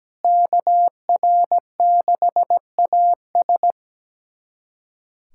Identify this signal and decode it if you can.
Morse code, heard as a keyed tone, about 17 wpm, 700 Hz. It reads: KR6AS
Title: USA Callsign DE Callsign 17wpm